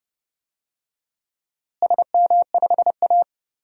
Morse code, heard as a keyed tone, about 30 wpm, 700 Hz. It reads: SM5A
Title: DXCC-100 Call Signs - 4 Characters - Repeated 3x - 30wpm